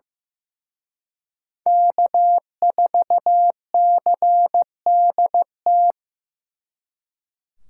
Morse code, heard as a keyed tone, about 18 wpm, 700 Hz. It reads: K4CDT
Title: Speed Racing - USA 1x3 Call Signs 15wpm